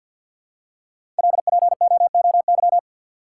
Morse code, heard as a keyed tone, about 51 wpm, 700 Hz. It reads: LPOOY
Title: Code Groups - 5 Letters - 50wpm